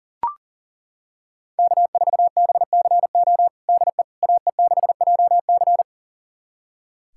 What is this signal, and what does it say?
Morse code, heard as a keyed tone, about 40 wpm, 700 Hz. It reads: K4BCO DE AE6JC
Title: USA Callsign DE Callsign - 5 Characters - 40wpm